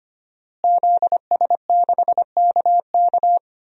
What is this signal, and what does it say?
Morse code, heard as a keyed tone, about 25 wpm, 700 Hz. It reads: ZS6KK
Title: DXCC-100 Call Signs - 5 Characters - Repeated 3x - 25wpm